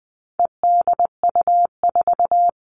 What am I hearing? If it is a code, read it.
Morse code, heard as a keyed tone, about 20 wpm, 700 Hz. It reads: EDU4
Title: All Cut Numbers - 4-Digit 20wpm